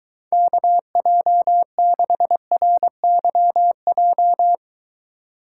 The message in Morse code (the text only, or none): KJ6RYJ